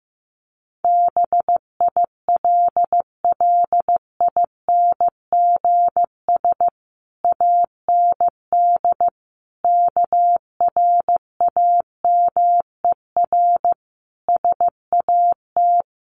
Morse code, heard as a keyed tone, about 15 wpm, 700 Hz. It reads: BILLINGS AND KRAMER SAT
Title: A Sound Of Thunder 15wpm